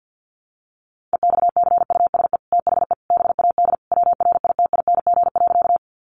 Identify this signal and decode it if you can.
Morse code, heard as a keyed tone, about 50 wpm, 700 Hz. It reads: EXCUSE THE BAD WRITING.
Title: Sentences from Top 800 Words 50wpm